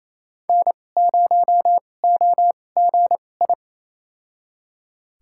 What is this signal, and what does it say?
Morse code, heard as a keyed tone, about 28 wpm, spaced as for 20 wpm, 700 Hz. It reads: N0OGI
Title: USA Call Signs - 5 Characters - 2x Character Spacing - 28wpm